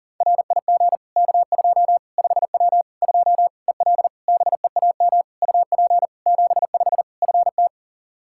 Morse code, heard as a keyed tone, about 40 wpm, 700 Hz. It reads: RIG K2 5W 2 EL BEAM UP 75 FT
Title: Practice QSO - Exchange 2 - Info - 40wpm